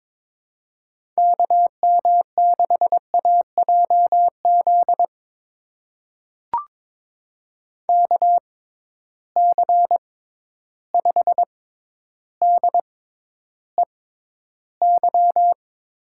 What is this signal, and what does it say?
Morse code, heard as a keyed tone, about 22 wpm, spaced as for 11 wpm, 700 Hz. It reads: KM6AJZ KC5DEY
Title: USA Call Signs - 6 Characters - 6x Character Spacing - 22wpm